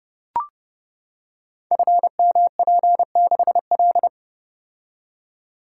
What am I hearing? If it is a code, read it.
Morse code, heard as a keyed tone, about 30 wpm, 700 Hz. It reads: FMP6L
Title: Code Groups - 5 Letter:Numbers - 30wpm